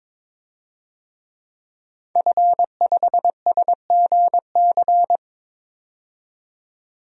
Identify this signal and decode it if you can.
Morse code, heard as a keyed tone, about 22 wpm, 700 Hz. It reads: F5SGC
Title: DXCC-100 Call Signs - 5 Characters - 22wpm